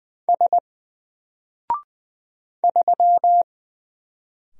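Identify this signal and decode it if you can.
Morse code, heard as a keyed tone, about 20 wpm, 700 Hz. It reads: S 3
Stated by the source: Single letter and number Morse drill at 20 wpm, from N4LCD.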